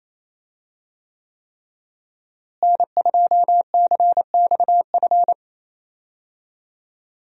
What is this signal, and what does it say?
Morse code, heard as a keyed tone, about 28 wpm, 700 Hz. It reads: N2CXF
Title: USA Call Signs - 5 Characters - Repeated 3x - 28wpm